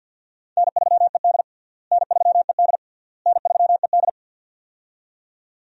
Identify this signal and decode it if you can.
Morse code, heard as a keyed tone, about 53 wpm, 700 Hz. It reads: N3ED N3ED N3ED
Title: USA Call Signs - 4 Characters - Repeated 3x - 50wpm